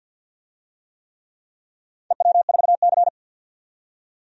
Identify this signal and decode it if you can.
Morse code, heard as a keyed tone, about 46 wpm, 700 Hz. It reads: EW4C
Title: DXCC-100 Call Signs - 4 Characters - 50wpm